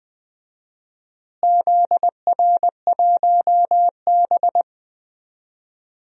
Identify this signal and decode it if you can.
Morse code, heard as a keyed tone, about 20 wpm, 700 Hz. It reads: ZR1B
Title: DXCC-100 Call Signs - 4 Characters - Repeated 2x - 20wpm